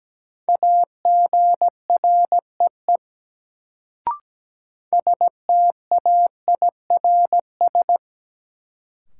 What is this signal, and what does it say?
Morse code, heard as a keyed tone, about 17 wpm, 700 Hz. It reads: AGREE STAIRS
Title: Sets of 5 Words Encore - New Word Drill - 17wpm